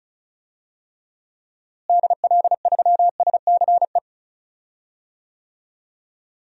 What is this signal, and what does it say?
Morse code, heard as a keyed tone, about 36 wpm, 700 Hz. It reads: DL3SCE
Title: DXCC-100 Call Signs - 6 Characters - Repeated 3x - 35wpm